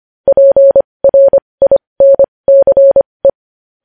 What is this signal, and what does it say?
Morse code, heard as a keyed tone, about 25 wpm, spaced as for 20 wpm, 550 Hz. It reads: PRINCE
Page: News Headlines in Morse Code at 20 WPM